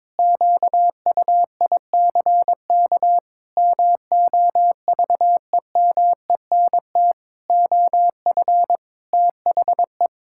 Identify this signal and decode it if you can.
Morse code, heard as a keyed tone, about 22 wpm, 700 Hz. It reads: QUICK MOVEMENT OF THE
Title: Pangram Challenge v2 - 22wpm